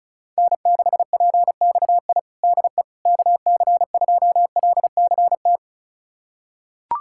Morse code, heard as a keyed tone, about 36 wpm, 700 Hz. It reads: N6PXI DE KC2LCT
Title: USA Callsign DE Callsign 35wpm